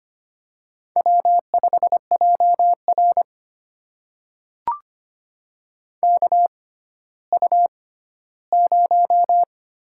Morse code, heard as a keyed tone, about 25 wpm, spaced as for 14 wpm, 700 Hz. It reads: W5JR KU0
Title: USA Call Signs - 4 Characters - 6x Character Spacing - 25wpm